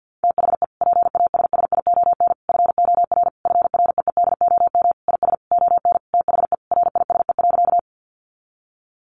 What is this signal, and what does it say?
Morse code, heard as a keyed tone, about 50 wpm, 700 Hz. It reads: THE PASSION FOR FREEDOM IS ON THE RISE.
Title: Sentences from Top 1300 Words 50wpm